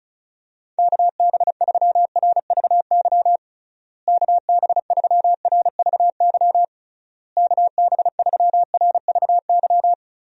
Morse code, heard as a keyed tone, about 36 wpm, 700 Hz. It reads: KB3RVY KB3RVY KB3RVY
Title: USA Call Signs - 6 Characters - Repeated 3x - 35wpm